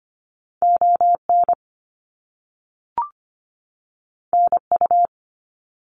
Morse code, heard as a keyed tone, about 25 wpm, 700 Hz. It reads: ON NU
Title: Canadian Provinces and Territories 25wpm